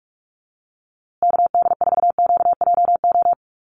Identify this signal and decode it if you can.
Morse code, heard as a keyed tone, about 44 wpm, 700 Hz. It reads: KD4QJO
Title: USA Call Signs Repeated 2x 45wpm